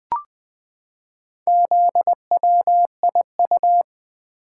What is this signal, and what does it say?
Morse code, heard as a keyed tone, about 20 wpm, 700 Hz. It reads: ZWIU
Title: Four Letter-Number Combinations - 20wpm